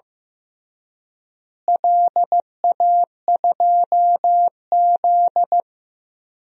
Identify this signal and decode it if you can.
Morse code, heard as a keyed tone, about 16 wpm, 700 Hz. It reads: LA2Z
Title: DXCC-100 Call Signs - 4 Characters - Repeated 3x - 15wpm